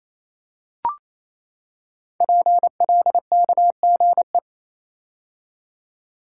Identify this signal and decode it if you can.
Morse code, heard as a keyed tone, about 27 wpm, 700 Hz. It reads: PLKGE
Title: Code Groups - 5 Letters - 28wpm